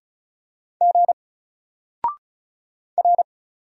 Morse code, heard as a keyed tone, about 36 wpm, 700 Hz. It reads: G R
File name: Single-Letter-Number-35wpm